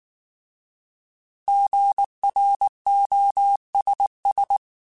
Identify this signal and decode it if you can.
Morse code, heard as a keyed tone, about 19 wpm, 800 Hz. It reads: GROSS